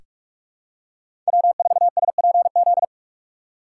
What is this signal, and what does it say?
Morse code, heard as a keyed tone, about 46 wpm, 700 Hz. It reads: W4SPZ
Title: USA Call Signs - 4x Character Spacing - 45wpm